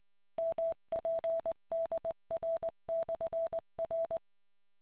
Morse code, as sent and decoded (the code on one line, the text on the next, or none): -- .--. -.. .-. -..-. .-.
MPDR/R